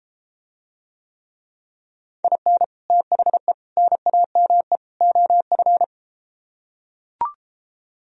Morse code, heard as a keyed tone, about 33 wpm, 700 Hz. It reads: IN THE NAME OF